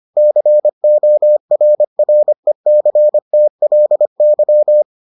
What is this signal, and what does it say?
Morse code, heard as a keyed tone, about 25 wpm, 600 Hz. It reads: CORRECTLY